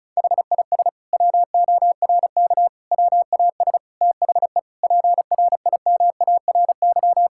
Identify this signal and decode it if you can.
Morse code, heard as a keyed tone, about 34 wpm, 700 Hz. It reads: HIS WORK WAS THE PRIMARY
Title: Sentences from Top 1000 Words - Review Part 5 - 35wpm